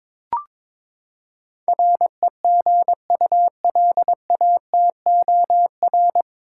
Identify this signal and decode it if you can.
Morse code, heard as a keyed tone, about 22 wpm, 700 Hz. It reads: REGULATOR
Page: Morse Code Ninja practice 22wpm